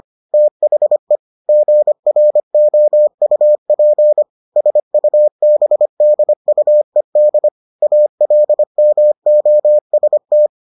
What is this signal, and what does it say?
Morse code, heard as a keyed tone, about 25 wpm, 600 Hz. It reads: THE GROUP SUBDUED ALMOST